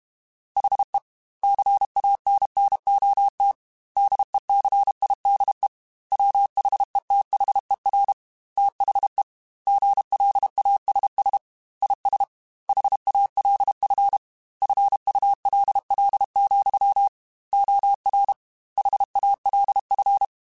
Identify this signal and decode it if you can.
Morse code, heard as a keyed tone, about 32 wpm, 800 Hz. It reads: HE CANNOT DECIDE WHETHER THE GLASS IS HALF FULL, OR HALF